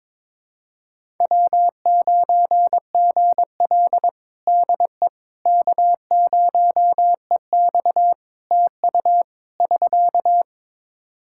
Morse code, heard as a keyed tone, about 21 wpm, 700 Hz. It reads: W9GL DE K0EX TU <SK>
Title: Practice QSO - Exchange 4 - Ending - 22wpm